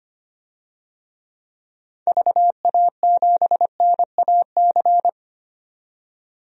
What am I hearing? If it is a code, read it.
Morse code, heard as a keyed tone, about 25 wpm, 700 Hz. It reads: VA7NAC
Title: DXCC-100 Call Signs - 6 Characters - 25wpm